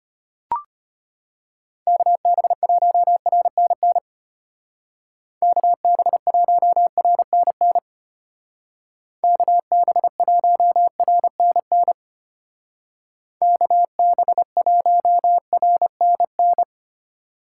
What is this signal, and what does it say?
Morse code, heard as a keyed tone, about 32 wpm, 700 Hz. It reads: KB1RNN KB1RNN KB1RNN KB1RNN